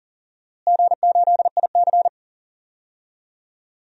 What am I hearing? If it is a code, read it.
Morse code, heard as a keyed tone, about 40 wpm, 700 Hz. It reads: G8IC
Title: DXCC-100 Call Signs - 4 Characters - Repeated 2x - 40wpm